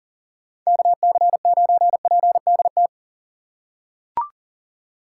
Morse code, heard as a keyed tone, about 40 wpm, 700 Hz. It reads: KC9PDT
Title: USA Call Signs - 6 Characters - 2x Character Spacing - 40wpm